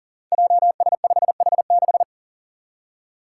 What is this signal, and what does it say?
Morse code, heard as a keyed tone, about 40 wpm, 700 Hz. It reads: JS5H6
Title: Five Challenging Letter-Number Combinations 40wpm